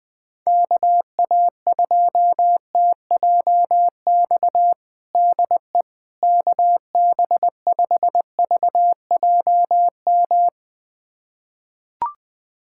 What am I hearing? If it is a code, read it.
Morse code, heard as a keyed tone, about 21 wpm, 700 Hz. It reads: KA2TJX DE KB5VJM
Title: USA Callsign DE Callsign 20wpm